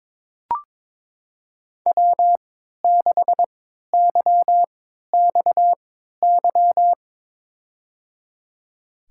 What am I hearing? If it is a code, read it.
Morse code, heard as a keyed tone, about 23 wpm, spaced as for 13 wpm, 700 Hz. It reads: W6YXY
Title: USA Call Signs - 5 Characters - 3x Character Spacing - 22wpm